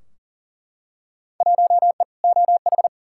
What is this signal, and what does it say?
Morse code, heard as a keyed tone, about 40 wpm, 700 Hz. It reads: 1E OH
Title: Field Day Exchanges - 40wpm